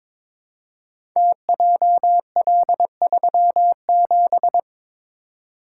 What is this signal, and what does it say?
Morse code, heard as a keyed tone, about 22 wpm, 700 Hz. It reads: TJL37
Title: Code Groups - 5 Letter:Numbers - 22wpm